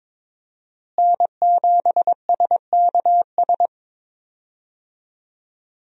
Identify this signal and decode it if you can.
Morse code, heard as a keyed tone, about 22 wpm, 700 Hz. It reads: N7SKS
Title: USA Call Signs - 5 Characters - Repeated 2x - 22wpm